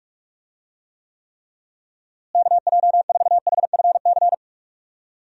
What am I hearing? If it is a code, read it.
Morse code, heard as a keyed tone, about 44 wpm, 700 Hz. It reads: KJ4HFC
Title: USA Call Signs - 6 Characters - Repeated 3x - 45wpm